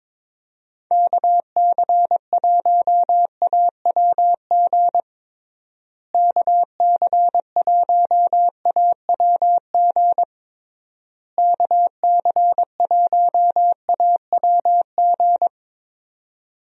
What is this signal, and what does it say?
Morse code, heard as a keyed tone, about 22 wpm, 700 Hz. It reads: KC1AWG KC1AWG KC1AWG
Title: USA Call Signs - 6 Characters - Repeated 3x - 22wpm